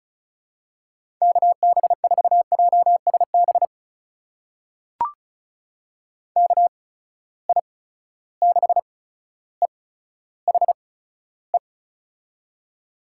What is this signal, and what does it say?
Morse code, heard as a keyed tone, about 36 wpm, spaced as for 15 wpm, 700 Hz. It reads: KB4JSB KI6EHE
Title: USA Call Signs - 6 Characters - 8x Character Spacing - 35wpm